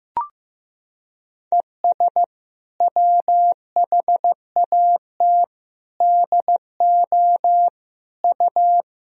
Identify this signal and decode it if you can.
Morse code, heard as a keyed tone, about 15 wpm, 700 Hz. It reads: ES WHAT DO U
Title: Sentences from Top 200 Words and Abbreviations 15wpm